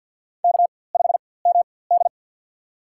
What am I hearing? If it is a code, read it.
Morse code, heard as a keyed tone, about 52 wpm, spaced as for 24 wpm, 700 Hz. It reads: K5MD
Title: USA Call Signs - 4x Character Spacing - 50wpm